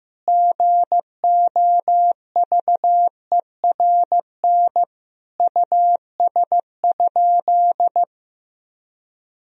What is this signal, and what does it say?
Morse code, heard as a keyed tone, about 15 wpm, 700 Hz. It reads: GOVERN US?
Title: Sentences from Top 1000 Words - Review Part 2 - 15wpm